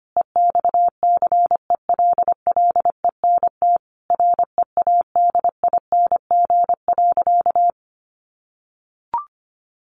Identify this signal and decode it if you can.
Morse code, heard as a keyed tone, about 25 wpm, 700 Hz. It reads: EXCELLENT READING.
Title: Sentences from Top 1000 Words 25wpm